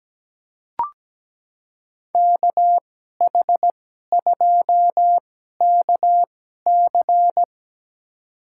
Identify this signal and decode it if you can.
Morse code, heard as a keyed tone, about 17 wpm, spaced as for 12 wpm, 700 Hz. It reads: KH2KC
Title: USA Call Signs - 5 Characters - 2x Character Spacing - 17wpm